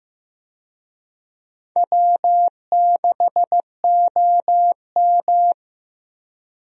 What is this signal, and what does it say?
Morse code, heard as a keyed tone, about 15 wpm, 700 Hz. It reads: W6OM
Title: USA Call Signs - 4 Characters - 15wpm